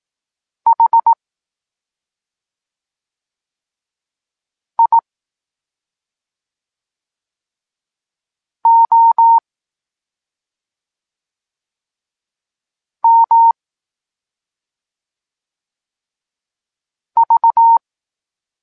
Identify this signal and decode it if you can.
Morse code, heard as a keyed tone, about 18 wpm, spaced as for 2 wpm, 920 Hz. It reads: HIOMV
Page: Tekst 8 – doskonalenie odbioru poznanych liter